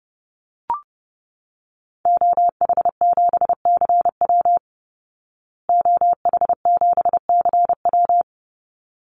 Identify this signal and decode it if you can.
Morse code, heard as a keyed tone, about 30 wpm, 700 Hz. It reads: OH7CW OH7CW
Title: Call Signs Repeated 3x - 30wpm